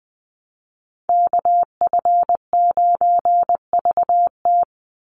Morse code, heard as a keyed tone, about 20 wpm, 700 Hz. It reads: KF9VT
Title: USA Call Signs 20wpm